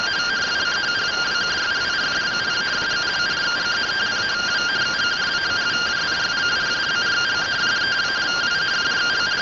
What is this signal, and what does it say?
Two-tone teleprinter signal, about 170 Hz shift.
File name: findings / CT SAGITAL 27.gen-14 rtty rx